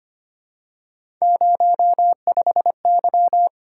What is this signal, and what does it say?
Morse code, heard as a keyed tone, about 25 wpm, 700 Hz. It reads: 05Y
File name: Three Letter-Number Combinations - 25wpm